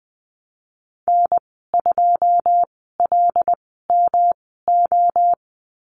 Morse code, heard as a keyed tone, about 19 wpm, spaced as for 14 wpm, 700 Hz. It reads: N2LMO
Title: 128 Lesson 21g - Number 6 - Call Signs 20wpm